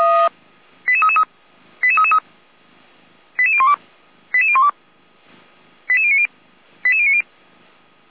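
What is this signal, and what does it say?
Five-tone selective call (ZVEI), heard as quick runs of five tones, tones of about 80 ms. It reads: 88333 88333 88012 88012 88090 88090
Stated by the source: ZVEI five-tone selective call 88409 received on an SDR.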